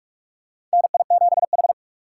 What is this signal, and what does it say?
Morse code, heard as a keyed tone, about 46 wpm, 700 Hz. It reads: NI7H
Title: USA Call Signs - 4 Characters - 3x Character Spacing - 45wpm